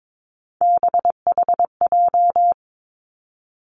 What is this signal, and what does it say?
Morse code, heard as a keyed tone, about 22 wpm, 700 Hz. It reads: BHJ
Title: Three Challenging Letter-Number Combinations 22wpm